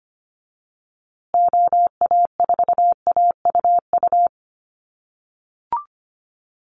Morse code, heard as a keyed tone, about 25 wpm, 700 Hz. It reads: OA4AUU
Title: USA Call Signs Repeated 2x 25wpm